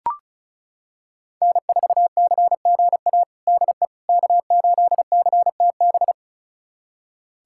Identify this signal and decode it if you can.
Morse code, heard as a keyed tone, about 36 wpm, 700 Hz. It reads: N4CGA DE K8CTB
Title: USA Callsign DE Callsign - 5 Characters - 35wpm